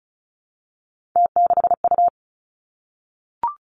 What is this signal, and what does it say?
Morse code, heard as a keyed tone, about 36 wpm, 700 Hz. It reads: T6U
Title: Three Challenging Letter-Number Combinations 35wpm